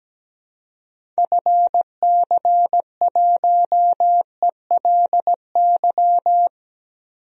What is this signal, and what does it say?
Morse code, heard as a keyed tone, about 17 wpm, 700 Hz. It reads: FC1ELY
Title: DXCC-100 Call Signs - 6 Characters - Repeated 2x - 17wpm